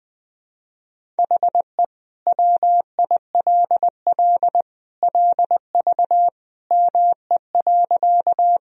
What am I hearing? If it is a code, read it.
Morse code, heard as a keyed tone, about 21 wpm, 700 Hz. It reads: HE WILL LV ME.
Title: Sentences from Top 400 Words and Abbreviations 20wpm